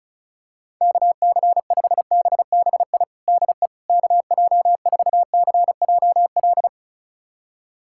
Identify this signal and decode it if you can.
Morse code, heard as a keyed tone, about 36 wpm, 700 Hz. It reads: KC5BBI DE KJ4CJL
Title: USA Callsign DE Callsign - 6 Characters - 35wpm